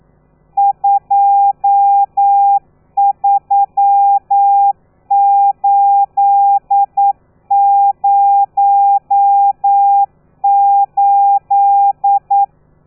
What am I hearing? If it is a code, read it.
Morse code, heard as a keyed tone, about 9 wpm, 800 Hz. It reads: 23808
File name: telegrafia / i2tfj / Lez12 40